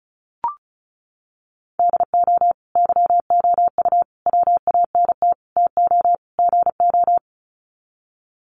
Morse code, heard as a keyed tone, about 34 wpm, 700 Hz. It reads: DO YOU WANT TO GO
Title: Sets of 5 Words Encore2 35wpm